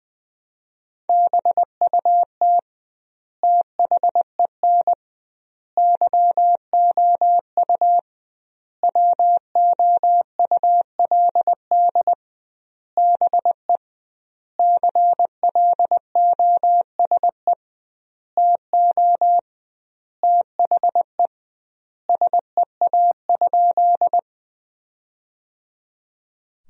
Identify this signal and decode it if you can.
Morse code, heard as a keyed tone, about 20 wpm, 700 Hz. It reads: BUT THEN YOU WOULD BE CLOSE TO THE SEA?
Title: 2x Word-Spacing - Sentences from Top 300 Words - 20wpm